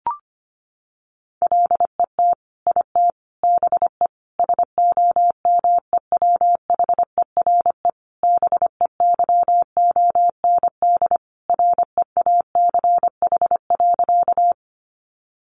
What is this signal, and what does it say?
Morse code, heard as a keyed tone, about 25 wpm, 700 Hz. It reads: LET IT BE SOMEWHERE BEYOND REACH.
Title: Sentences from Top 1400 Words 25wpm